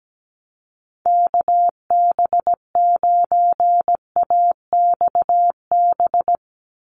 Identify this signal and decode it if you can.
Morse code, heard as a keyed tone, about 17 wpm, 700 Hz. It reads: KB9AXB
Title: USA Call Signs 17wpm